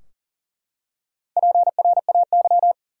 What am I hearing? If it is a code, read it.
Morse code, heard as a keyed tone, about 40 wpm, 700 Hz. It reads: PRAY
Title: Four Letter Words - 40wpm